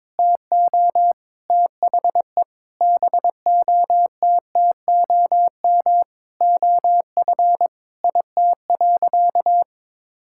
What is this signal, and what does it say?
Morse code, heard as a keyed tone, about 22 wpm, 700 Hz. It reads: TO THE BOTTOM OF IT.
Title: Sentences from Top 1000 Words - Review Part 6 - 22wpm